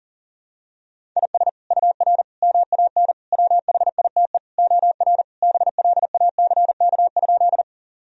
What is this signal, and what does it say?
Morse code, heard as a keyed tone, about 40 wpm, 700 Hz. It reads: IS UR MAN WHITE OR BLACK?
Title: Sentences from Top 400 Words and Abbreviations 40wpm